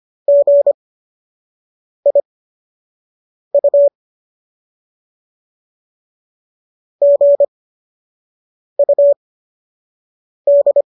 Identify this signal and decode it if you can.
Morse code, heard as a keyed tone, about 25 wpm, spaced as for 6 wpm, 570 Hz. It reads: GIU GUD